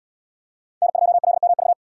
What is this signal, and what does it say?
Morse code, heard as a keyed tone, about 77 wpm, 700 Hz. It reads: N2LMV